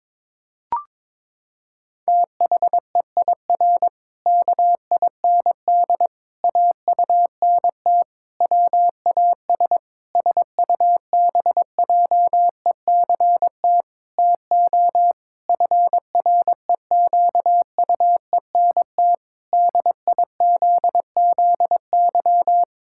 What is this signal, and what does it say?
Morse code, heard as a keyed tone, about 22 wpm, 700 Hz. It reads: THEIR KIND AUNT WAS SUBJECT TO FREQUENT DIZZY
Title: Pangram Challenge v2 - 22wpm